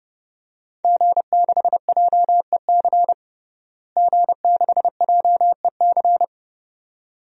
Morse code, heard as a keyed tone, about 29 wpm, 700 Hz. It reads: G6JEC G6JEC
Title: DXCC-100 Call Signs - 5 Characters - Repeated 2x - 30wpm